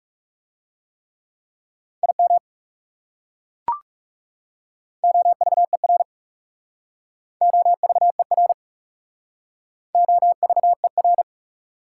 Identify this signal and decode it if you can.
Morse code, heard as a keyed tone, about 38 wpm, 700 Hz. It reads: IM OVER OVER OVER
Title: Speed-Racing Prefixes and Suffixes 30wpm